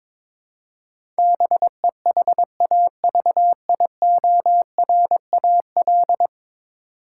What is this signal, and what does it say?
Morse code, heard as a keyed tone, about 22 wpm, 700 Hz. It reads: BEHAVIORAL
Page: morse-code-ninja-practice-22wpm